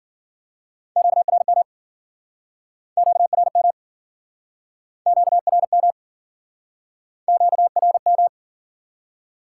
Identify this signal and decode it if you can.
Morse code, heard as a keyed tone, about 54 wpm, 700 Hz. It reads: QRM QRM QRM QRM